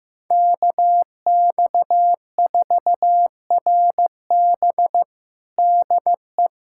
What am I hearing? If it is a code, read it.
Morse code, heard as a keyed tone, about 15 wpm, 700 Hz. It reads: KX4RB DE
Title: USA Callsign DE Callsign - 5 Characters - 15wpm